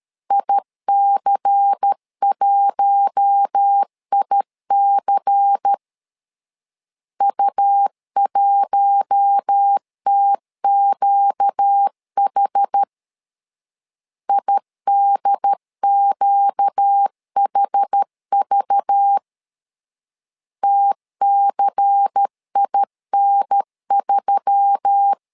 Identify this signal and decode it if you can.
Morse code, heard as a keyed tone, about 13 wpm, 800 Hz. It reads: IC1IC U1TQH IDQHV TCIN3
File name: LEZIONI / 5 LBVC / 5 LBVC 12wpm 1955